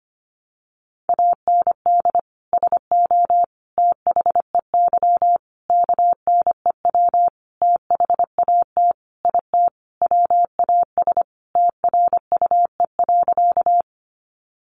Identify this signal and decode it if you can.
Morse code, heard as a keyed tone, about 25 wpm, 700 Hz. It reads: AND SO THEY KNEW THAT IT WAS TRUE.
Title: Sentences from Top 400 Words 25wpm